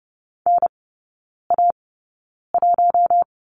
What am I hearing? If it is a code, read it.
Morse code, heard as a keyed tone, about 30 wpm, spaced as for 9 wpm, 700 Hz. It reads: NA1
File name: Lesson 19e - Letter w - Three Characters 30wpm